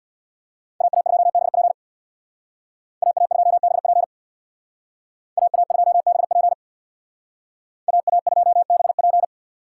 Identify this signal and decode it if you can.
Morse code, heard as a keyed tone, about 63 wpm, 700 Hz. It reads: AA2BP AA2BP AA2BP AA2BP